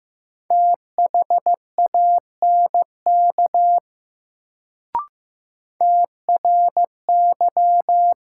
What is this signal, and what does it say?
Morse code, heard as a keyed tone, about 15 wpm, 700 Hz. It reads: THANK TRY
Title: Sets of 2 Words - New Word Drill - 15wpm